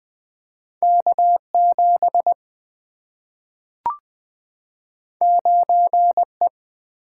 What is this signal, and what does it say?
Morse code, heard as a keyed tone, about 20 wpm, 700 Hz. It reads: K7 9E
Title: Two Letter-Number Combinations - 20wpm